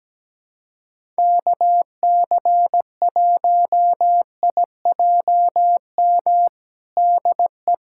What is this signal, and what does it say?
Morse code, heard as a keyed tone, about 17 wpm, 700 Hz. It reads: KC1IJM DE
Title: USA Callsign DE Callsign - 6 Characters - 17wpm